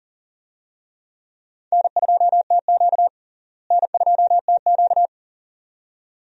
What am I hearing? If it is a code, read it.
Morse code, heard as a keyed tone, about 37 wpm, 700 Hz. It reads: N2TQ N2TQ
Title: USA Call Signs - 4 Characters - Repeated 2x - 40wpm